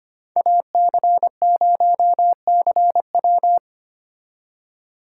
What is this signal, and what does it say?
Morse code, heard as a keyed tone, about 33 wpm, 700 Hz. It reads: AC0CW